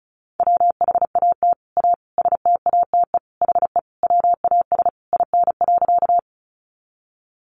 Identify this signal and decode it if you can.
Morse code, heard as a keyed tone, about 34 wpm, 700 Hz. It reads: WHAT A STATE HE WAS IN.
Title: Sentences from Top 400 Words 35wpm